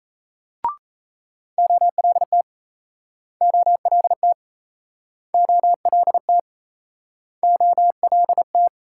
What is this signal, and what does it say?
Morse code, heard as a keyed tone, about 38 wpm, 700 Hz. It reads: OLT OLT OLT OLT